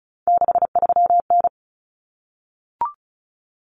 Morse code, heard as a keyed tone, about 34 wpm, 700 Hz. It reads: 63N
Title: Common Cut Numbers - 3-Digit 35wpm